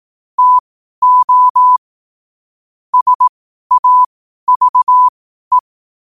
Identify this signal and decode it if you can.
Morse code, heard as a keyed tone, about 18 wpm, spaced as for 12 wpm, 1000 Hz. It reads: TO SAVE
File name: qotdfiles / 12wpm